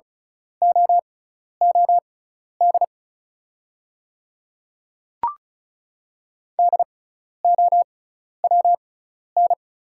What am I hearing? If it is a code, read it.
Morse code, heard as a keyed tone, about 36 wpm, spaced as for 12 wpm, 700 Hz. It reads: OOD DOWN
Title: 6x Character Spacing - Top 200 Words - 35wpm